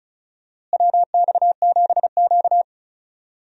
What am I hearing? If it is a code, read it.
Morse code, heard as a keyed tone, about 34 wpm, 700 Hz. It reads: WX7Q
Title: USA Call Signs - 4 Characters - Repeated 3x - 35wpm